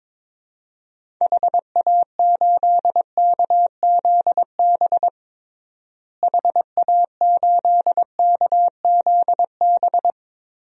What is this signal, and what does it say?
Morse code, heard as a keyed tone, about 22 wpm, 700 Hz. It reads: HA8KZB HA8KZB
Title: DXCC-100 Call Signs - 6 Characters - Repeated 3x - 22wpm